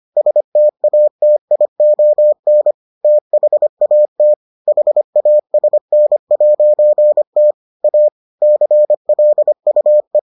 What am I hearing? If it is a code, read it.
Morse code, heard as a keyed tone, about 25 wpm, 600 Hz. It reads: STATION THAT HASN'T A CLUE